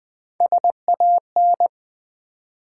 Morse code, heard as a keyed tone, about 20 wpm, 700 Hz. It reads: SAN
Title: Build A Word - 3 Letters At A Time - 8-Letter Words - 20wpm